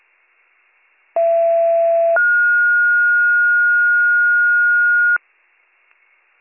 Two-tone page, heard as a one-second tone, then a three-second tone, about 670 then 1430 Hz. It is heard from 1.2 to 5.2 s.